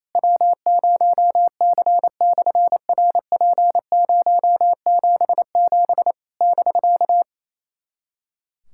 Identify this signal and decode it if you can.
Morse code, heard as a keyed tone, about 28 wpm, 700 Hz. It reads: W0C/RP077 <BK>
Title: SOTA QSO Practice 28wpm